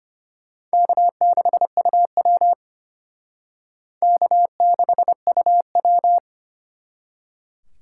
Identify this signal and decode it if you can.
Morse code, heard as a keyed tone, about 34 wpm, 700 Hz. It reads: K6UW K6UW